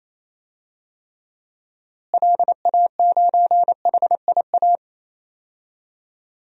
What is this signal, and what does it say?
Morse code, heard as a keyed tone, about 28 wpm, 700 Hz. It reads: LA9HIA
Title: DXCC-100 Call Signs - 6 Characters - Repeated 3x - 28wpm